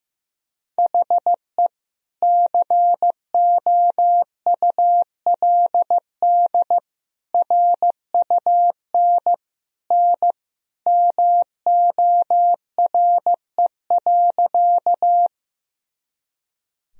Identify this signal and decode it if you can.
Morse code, heard as a keyed tone, about 15 wpm, 700 Hz. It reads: HE COULD RUN N MORE.
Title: Sentences from Top 300 Words and Abbreviations 15wpm